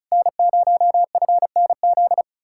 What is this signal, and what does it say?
Morse code, heard as a keyed tone, about 46 wpm, spaced as for 41 wpm, 700 Hz. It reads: N0FNZ
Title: Speed-Racing Common QSO Phrases - 35wpm